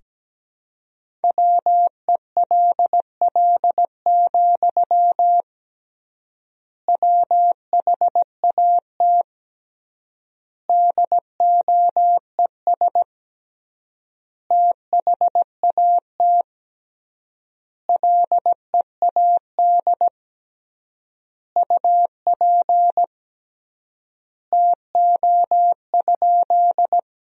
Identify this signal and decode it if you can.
Morse code, heard as a keyed tone, about 17 wpm, 700 Hz. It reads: WELL, WHAT DOES THAT LEAD UP TO?
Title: 3x Word-Spacing - Sentences from Top 500 Words - 17wpm